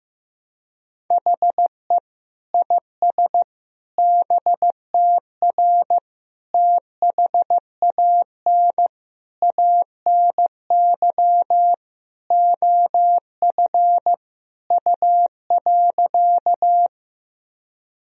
Morse code, heard as a keyed tone, about 16 wpm, 700 Hz. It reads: HE IS BTR THAN ANY OF U.
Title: Sentences from Top 400 Words and Abbreviations 15wpm